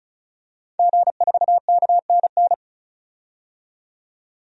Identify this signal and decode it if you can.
Morse code, heard as a keyed tone, about 35 wpm, 700 Hz. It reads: G4KNN
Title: DXCC-100 Call Signs - 5 Characters - 35wpm